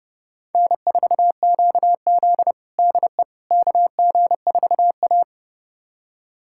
Morse code, heard as a keyed tone, about 32 wpm, 700 Hz. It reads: N4QZ DE KG4A
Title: USA Callsign DE Callsign - 4 Characters - 30wpm